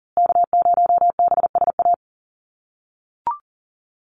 Morse code, heard as a keyed tone, about 40 wpm, 700 Hz. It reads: K0BSA
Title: USA Call Signs Repeated 3x 40wpm